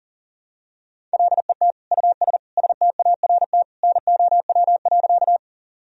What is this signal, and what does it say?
Morse code, heard as a keyed tone, about 40 wpm, 700 Hz. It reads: LET US START NOW.
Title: Sentences from Top 1000 Words - Review Part 1 - 40wpm